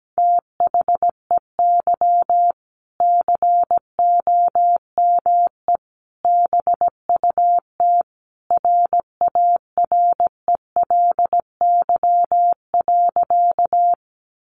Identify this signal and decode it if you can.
Morse code, heard as a keyed tone, about 17 wpm, 700 Hz. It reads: THEY COME BUT RARELY.
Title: Sentences from Top 1400 Words 17wpm